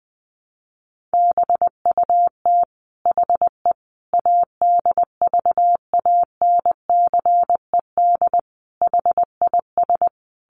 Morse code, heard as a keyed tone, about 20 wpm, 700 Hz. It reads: BUT HE ADVANCED HIS
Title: Sentences from Top 1200 Words 20wpm